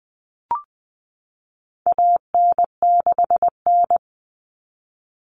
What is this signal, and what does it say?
Morse code, heard as a keyed tone, about 20 wpm, 700 Hz. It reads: AN6N
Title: Common Cut Numbers - 4-Digit 20wpm